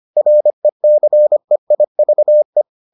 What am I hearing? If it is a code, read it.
Morse code, heard as a keyed tone, about 25 wpm, 600 Hz. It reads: RECEIVE